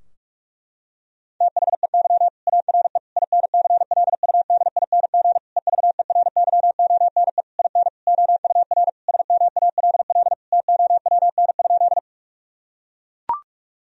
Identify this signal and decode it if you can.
Morse code, heard as a keyed tone, about 41 wpm, 700 Hz. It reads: THEY ARE INCLUDING EVERYONE IN OUR SMALL TOWN?
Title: Sentences from Top 1000 Words - Review Part 4 - 45wpm